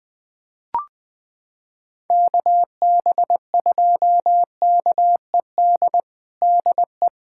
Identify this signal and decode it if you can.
Morse code, heard as a keyed tone, about 20 wpm, 700 Hz. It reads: KB2KED DE
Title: USA Callsign DE Callsign 20wpm